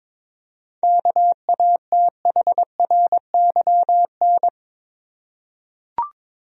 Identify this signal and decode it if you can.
Morse code, heard as a keyed tone, about 22 wpm, 700 Hz. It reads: KATHRYN
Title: USA Common Names - 22wpm